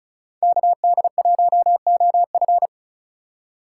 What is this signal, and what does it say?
Morse code, heard as a keyed tone, about 41 wpm, 700 Hz. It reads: KD1OF
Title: Speed Racing - USA 2x2 Call Signs 35wpm